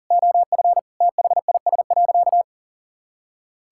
Morse code, heard as a keyed tone, about 40 wpm, 700 Hz. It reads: OF THIS.
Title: Sentences from Top 1000 Words - Review Part 6 - 40wpm